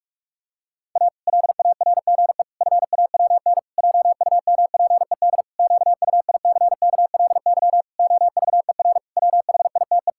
Morse code, heard as a keyed tone, about 46 wpm, 700 Hz. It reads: A LARGE FAWN JUMPED QUICKLY OVER WHITE
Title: Pangram Challenge v2 - 45wpm